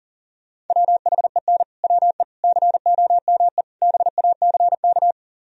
Morse code, heard as a keyed tone, about 50 wpm, 700 Hz. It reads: WHEN WE COME BACK
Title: Speed-Racing - Sets of 4 Words 40wpm